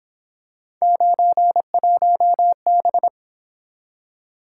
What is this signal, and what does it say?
Morse code, heard as a keyed tone, about 27 wpm, 700 Hz. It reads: 91B
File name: Speed-Racing - Three Letter-Number Combinations - 22wpm